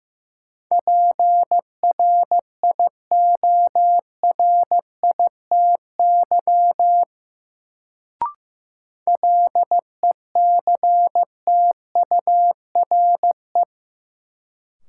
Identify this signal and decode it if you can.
Morse code, heard as a keyed tone, about 15 wpm, 700 Hz. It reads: PRIORITY LECTURE
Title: Sentences from Top 1600 Words - New Word Drill Alt - 15wpm